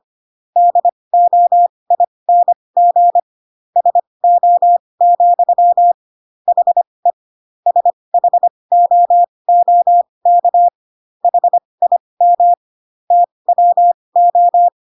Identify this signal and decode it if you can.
Morse code, heard as a keyed tone, about 25 wpm, spaced as for 20 wpm, 700 Hz. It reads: DOING SO, HE SHOOK HIM TWO